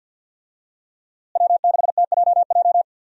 Morse code, heard as a keyed tone, about 52 wpm, 700 Hz. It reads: W6TJJ
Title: USA Call Signs - 5 Characters - Repeated 3x - 50wpm